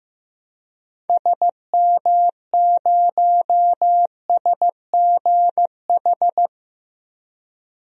Morse code, heard as a keyed tone, about 15 wpm, 700 Hz. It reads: SM0SGH
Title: DXCC-100 Call Signs - 6 Characters - Repeated 2x - 15wpm